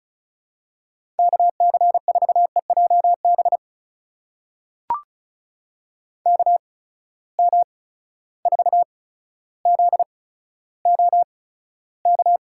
Morse code, heard as a keyed tone, about 36 wpm, spaced as for 15 wpm, 700 Hz. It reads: KC4EJB KM4ZOK
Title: USA Call Signs - 6 Characters - 8x Character Spacing - 35wpm